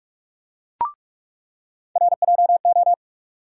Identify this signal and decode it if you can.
Morse code, heard as a keyed tone, about 47 wpm, 700 Hz. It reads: RJO
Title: USA Call Sign Postfixes - 3 Letters 45wpm